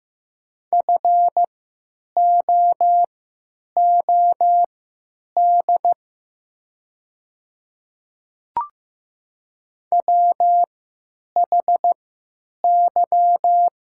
Morse code, heard as a keyed tone, about 15 wpm, spaced as for 9 wpm, 700 Hz. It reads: FOOD WHY
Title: 3x Character Spacing - Top 500 Words - 15wpm